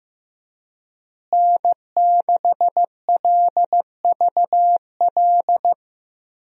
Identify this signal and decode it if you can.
Morse code, heard as a keyed tone, about 16 wpm, 700 Hz. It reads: N6LVL
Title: USA Call Signs - 5 Characters - Repeated 2x - 15wpm